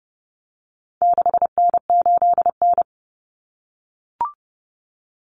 Morse code, heard as a keyed tone, about 30 wpm, 700 Hz. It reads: 6N8N
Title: Common Cut Numbers - 4-Digit 30wpm